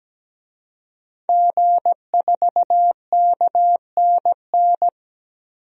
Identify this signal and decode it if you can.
Morse code, heard as a keyed tone, about 17 wpm, 700 Hz. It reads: G4KNN
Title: DXCC-100 Call Signs - 5 Characters - 17wpm